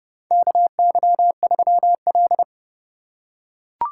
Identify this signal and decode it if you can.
Morse code, heard as a keyed tone, about 30 wpm, 700 Hz. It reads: KY3L
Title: USA Call Signs - 4 Characters - 8x Character Spacing - 30wpm